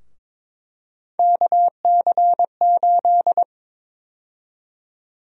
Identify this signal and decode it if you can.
Morse code, heard as a keyed tone, about 22 wpm, 700 Hz. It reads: KC8
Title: USA Call Sign Prefixes - Repeated 2x - 22wpm